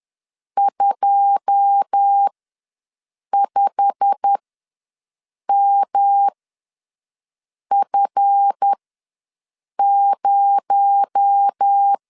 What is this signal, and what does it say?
Morse code, heard as a keyed tone, about 10 wpm, spaced as for 5 wpm, 800 Hz. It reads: 25MF0
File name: LEZIONI / 9 2 ws 10wpm 4130